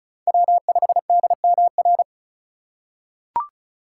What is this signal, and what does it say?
Morse code, heard as a keyed tone, about 36 wpm, 700 Hz. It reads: W5DMR
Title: USA Call Signs - 5 Characters - 3x Character Spacing - 35wpm